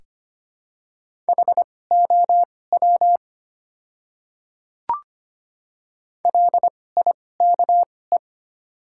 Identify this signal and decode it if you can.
Morse code, heard as a keyed tone, about 25 wpm, spaced as for 18 wpm, 700 Hz. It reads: HOW LIKE
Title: 2x Character Spacing - Top 200 Words - 25wpm